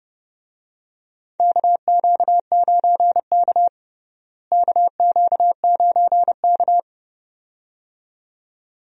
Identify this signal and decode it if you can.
Morse code, heard as a keyed tone, about 30 wpm, 700 Hz. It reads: KQ9K KQ9K
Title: USA Call Signs - 4 Characters - Repeated 2x - 30wpm